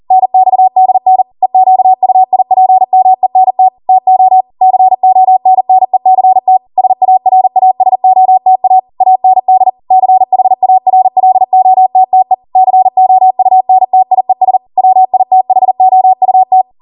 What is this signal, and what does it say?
Morse code, heard as a keyed tone, about 40 wpm, 750 Hz. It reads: NXDN EQUIPMENT TO CONNECT SARASOTA AND CHARLOTTE COUNTIES WITHOUT